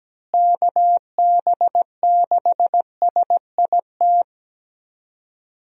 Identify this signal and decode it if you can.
Morse code, heard as a keyed tone, about 17 wpm, 700 Hz. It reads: KB6SIT
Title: USA Call Signs - 6 Characters - Repeated 3x - 17wpm